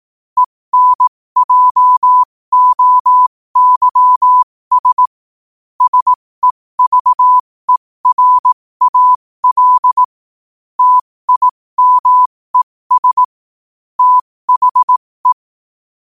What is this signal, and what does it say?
Morse code, heard as a keyed tone, about 18 wpm, spaced as for 15 wpm, 1000 Hz. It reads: ENJOYS SEVERAL TIMES THE
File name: qotdfiles / 15wpm